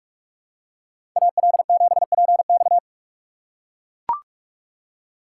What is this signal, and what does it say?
Morse code, heard as a keyed tone, about 44 wpm, 690 Hz. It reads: AL7PX
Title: USA Call Signs - 5 Characters - 8x Character Spacing - 45wpm